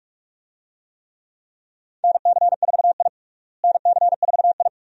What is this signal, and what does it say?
Morse code, heard as a keyed tone, about 41 wpm, 700 Hz. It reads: NC4I NC4I
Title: USA Call Signs - 4 Characters - Repeated 2x - 45wpm